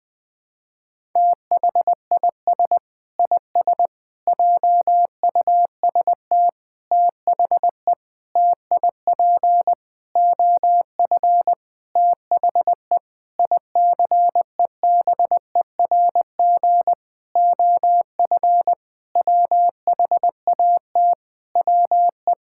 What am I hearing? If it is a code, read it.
Morse code, heard as a keyed tone, about 20 wpm, 700 Hz. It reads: THIS IS JUST THE TIP OF THE ICEBERG OF WHAT WE
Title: Sentences with 8-Word Phrases - Spoken Beginning - 20wpm